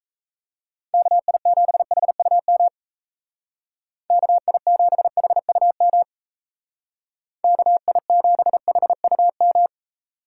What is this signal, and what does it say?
Morse code, heard as a keyed tone, about 35 wpm, 700 Hz. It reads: KI7HUM KI7HUM KI7HUM